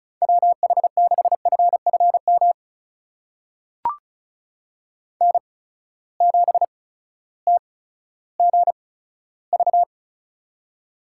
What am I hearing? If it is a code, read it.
Morse code, heard as a keyed tone, about 36 wpm, spaced as for 16 wpm, 700 Hz. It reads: WH6FFM N7TGV
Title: USA Call Signs - 8x Character Spacing - 35wpm